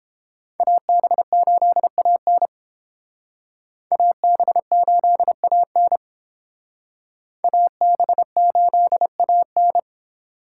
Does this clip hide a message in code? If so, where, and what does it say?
Yes: Morse code from 0.6 to 9.8 s, heard as a keyed tone, about 29 wpm, 700 Hz. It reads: AB8AN AB8AN AB8AN